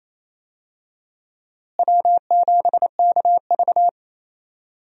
Morse code, heard as a keyed tone, about 28 wpm, 700 Hz. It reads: W7KV
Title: USA Call Signs - 4 Characters - Repeated 2x - 28wpm